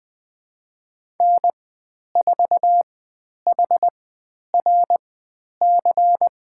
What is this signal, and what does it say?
Morse code, heard as a keyed tone, about 20 wpm, spaced as for 10 wpm, 700 Hz. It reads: N4HRC
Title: 095 Lesson 16g - Letter c - Call Signs 20wpm